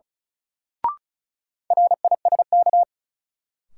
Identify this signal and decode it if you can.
Morse code, heard as a keyed tone, about 36 wpm, 700 Hz. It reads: RISK
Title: Sentences from Top 600 Words - New Word Drill - 35wpm